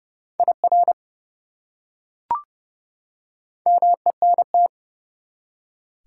Morse code, heard as a keyed tone, about 29 wpm, 700 Hz. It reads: IR MENT
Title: Prefixes and Suffixes 30wpm